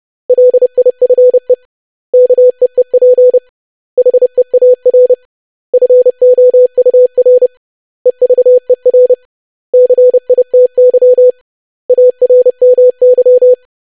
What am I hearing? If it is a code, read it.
Morse code, heard as a keyed tone, about 30 wpm, 500 Hz. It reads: LIFE KEEP HEAR FOUR EVER CITY ARMY
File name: lesson / web / basic500-25wpm